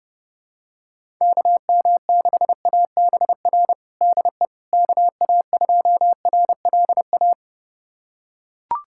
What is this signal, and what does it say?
Morse code, heard as a keyed tone, about 30 wpm, 700 Hz. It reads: KM6ABR DE KA2RLA
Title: USA Callsign DE Callsign 30wpm